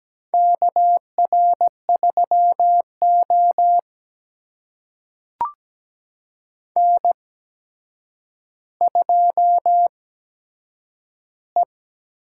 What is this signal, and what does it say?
Morse code, heard as a keyed tone, about 17 wpm, spaced as for 8 wpm, 700 Hz. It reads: KR3O N2E
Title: USA Call Signs - 4 Characters - 8x Character Spacing - 17wpm